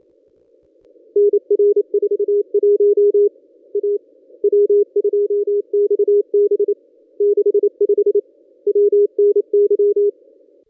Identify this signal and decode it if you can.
Morse code, heard as a keyed tone, about 28 wpm, 410 Hz. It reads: NR 41 A W2XB 65 WNY
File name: contests / ss-cw-k5tr-op-n5rz -1200